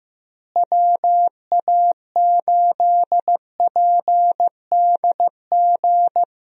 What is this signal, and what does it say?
Morse code, heard as a keyed tone, about 15 wpm, 700 Hz. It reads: WA8PDG
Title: USA Call Signs - 6 Characters - Repeated 3x - 15wpm